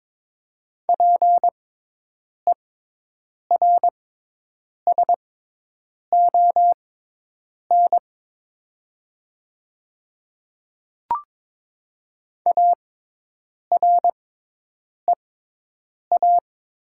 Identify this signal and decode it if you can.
Morse code, heard as a keyed tone, about 22 wpm, spaced as for 8 wpm, 700 Hz. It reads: PERSON AREA
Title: 6x Character Spacing - Top 500 Words - 22wpm